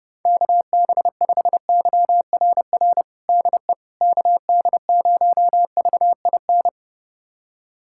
Morse code, heard as a keyed tone, about 30 wpm, 700 Hz. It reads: KB5YRR DE KD0VIN
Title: USA Callsign DE Callsign 30wpm